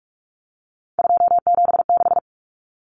Morse code, heard as a keyed tone, about 44 wpm, 700 Hz. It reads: 276
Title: Numbers - 3-Digit 45wpm